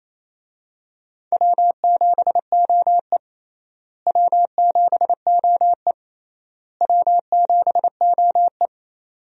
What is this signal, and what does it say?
Morse code, heard as a keyed tone, about 28 wpm, 700 Hz. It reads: W7OE W7OE W7OE
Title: USA Call Signs - 4 Characters - Repeated 3x - 28wpm